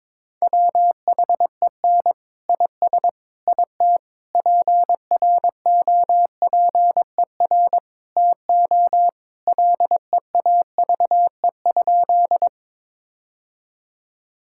Morse code, heard as a keyed tone, about 22 wpm, 700 Hz. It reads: WHEN IS IT PROPER TO LEAVE?
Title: Sentences from Top 1000 Words - Review Part 5 - 22wpm